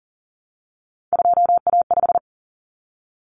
Morse code, heard as a keyed tone, about 40 wpm, 700 Hz. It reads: 2A5
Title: Common Cut Numbers - 3-Digit 40wpm